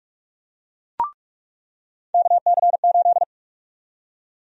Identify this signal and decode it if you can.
Morse code, heard as a keyed tone, about 47 wpm, 700 Hz. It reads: KC8
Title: USA Call Sign Prefixes - 45wpm